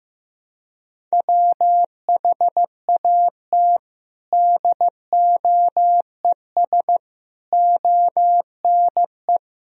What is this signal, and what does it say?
Morse code, heard as a keyed tone, about 15 wpm, 700 Hz. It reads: WHAT DOES ONE
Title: Sentences from Top 200 Words and Abbreviations 15wpm